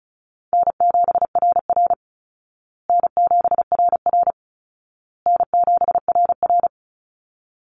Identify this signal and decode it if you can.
Morse code, heard as a keyed tone, about 34 wpm, 700 Hz. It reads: N7RR N7RR N7RR
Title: Easy Call Signs Repeated 3x - 35wpm